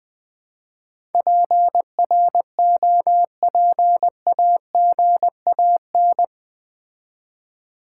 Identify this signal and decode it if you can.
Morse code, heard as a keyed tone, about 20 wpm, 700 Hz. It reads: PROPAGAN
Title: Build A Word - 4 Letters At A Time - 10-Letter Words - 20wpm